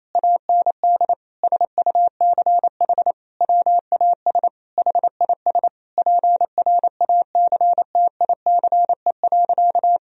Morse code, heard as a keyed tone, about 28 wpm, 700 Hz. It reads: AND SUCH WAS HIS PRACTICE.
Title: Sentences from Top 1000 Words - Review Part 4 - 28wpm